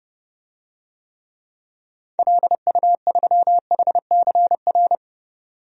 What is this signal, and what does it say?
Morse code, heard as a keyed tone, about 32 wpm, 700 Hz. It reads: LU3HCR
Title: DXCC-100 Call Signs - 6 Characters - 30wpm